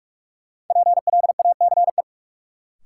Morse code, heard as a keyed tone, about 45 wpm, 700 Hz. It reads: PLACE
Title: Sets of 4 Words - New Word Drill - 45wpm